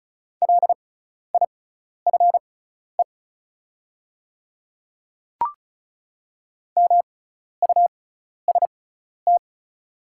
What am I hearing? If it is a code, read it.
Morse code, heard as a keyed tone, about 36 wpm, spaced as for 12 wpm, 690 Hz. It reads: LIFE MUST
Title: 6x Character Spacing - Top 300 Words - 35wpm